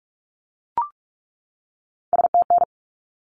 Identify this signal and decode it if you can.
Morse code, heard as a keyed tone, about 44 wpm, 700 Hz. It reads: STN
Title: Abbreviations 45wpm